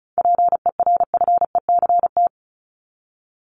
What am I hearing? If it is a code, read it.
Morse code, heard as a keyed tone, about 34 wpm, 700 Hz. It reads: PERFECT
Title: LY Suffix - Top 100 Words with Base Words 35wpm